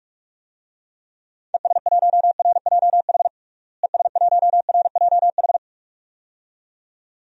Morse code, heard as a keyed tone, about 46 wpm, 700 Hz. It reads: ES1RJH ES1RJH
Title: DXCC-100 Call Signs - 6 Characters - Repeated 2x - 45wpm